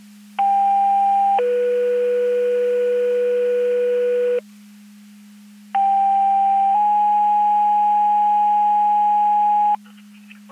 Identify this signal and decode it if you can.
Two-tone page, heard as a one-second tone, then a three-second tone, about 800 then 490 Hz, about 790 then 830 Hz. It is heard from 0.4 to 9.7 s.
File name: Portage 154.385 Paging